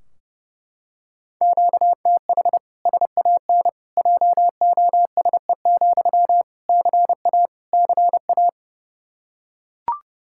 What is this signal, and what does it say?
Morse code, heard as a keyed tone, about 31 wpm, 700 Hz. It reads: QTH SAN JOSE, CA CA
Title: Practice QSO - Exchange 1 - Beginning - 30wpm